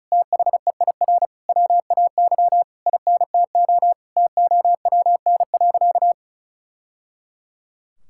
Morse code, heard as a keyed tone, about 36 wpm, 700 Hz. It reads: THEIR WAY INTO TOWN.
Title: Sentences from Top 1000 Words - Review Part 6 - 35wpm